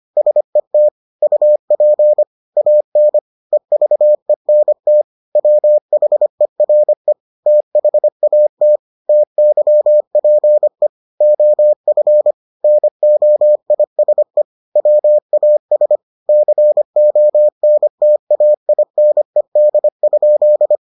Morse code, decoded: SET UP AN EVENT WHERE THAT TYPE OF NOISE WAS CONTAINED?

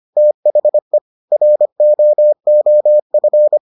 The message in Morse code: THE ROOF

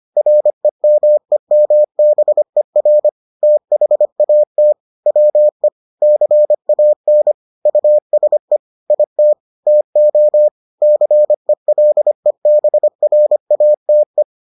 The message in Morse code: REMEMBER THAT WE CAN USE IT TO CELEBRATE

